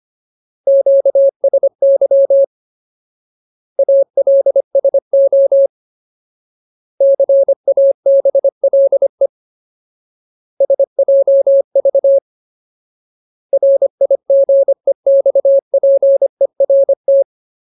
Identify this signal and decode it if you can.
Morse code, heard as a keyed tone, about 25 wpm, 550 Hz. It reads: QSY ALSO CABLE SJV RIGEXPERT
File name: Ultimate Word List / 25WPM Ultimate